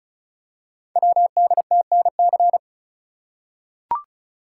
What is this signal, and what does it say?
Morse code, heard as a keyed tone, about 36 wpm, 690 Hz. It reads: WDTNC
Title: Code Groups - 5 Letters - 35wpm